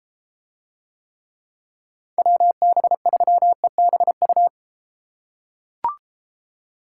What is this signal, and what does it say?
Morse code, heard as a keyed tone, about 31 wpm, 700 Hz. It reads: WB3EBU